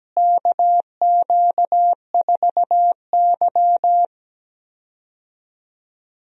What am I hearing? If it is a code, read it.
Morse code, heard as a keyed tone, about 17 wpm, 700 Hz. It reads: KQ4Y
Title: USA Call Signs - 4 Characters - Repeated 2x - 17wpm